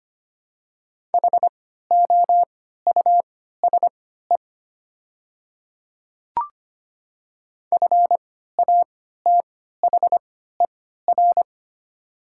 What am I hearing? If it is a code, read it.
Morse code, heard as a keyed tone, about 25 wpm, spaced as for 14 wpm, 700 Hz. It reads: HOUSE FATHER
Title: 3x Character Spacing - Top 500 Words - 25wpm